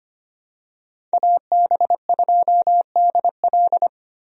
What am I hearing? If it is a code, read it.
Morse code, heard as a keyed tone, about 25 wpm, 700 Hz. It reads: AB2DL
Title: USA Call Signs - 5 Characters - Repeated 2x - 25wpm